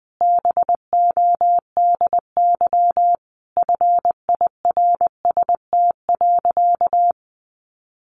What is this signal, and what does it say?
Morse code, heard as a keyed tone, about 20 wpm, 700 Hz. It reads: BODY FIRST.
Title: Sentences from Top 1400 Words 20wpm